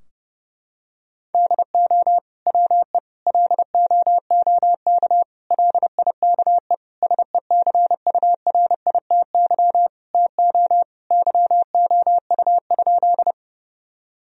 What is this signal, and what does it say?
Morse code, heard as a keyed tone, about 30 wpm, 700 Hz. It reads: DO WE LOOK LIKE SECURITY TO YOU?
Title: Sentences from Top 1000 Words - Review Part 4 - 30wpm